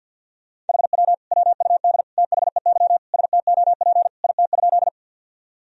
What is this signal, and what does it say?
Morse code, heard as a keyed tone, about 52 wpm, 700 Hz. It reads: HW WUD THEY STOP IT?
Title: Sentences from Top 300 Words and Abbreviations 50wpm